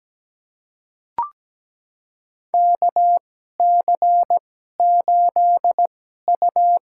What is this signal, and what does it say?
Morse code, heard as a keyed tone, about 17 wpm, spaced as for 12 wpm, 700 Hz. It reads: KC8U
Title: USA Call Signs - 2x Character Spacing - 17wpm